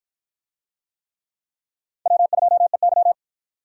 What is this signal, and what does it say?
Morse code, heard as a keyed tone, about 53 wpm, 700 Hz. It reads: W2EY